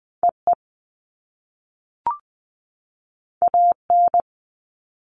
Morse code, heard as a keyed tone, about 20 wpm, 700 Hz. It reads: EE AN
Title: All Cut Numbers - 2-Digit 20wpm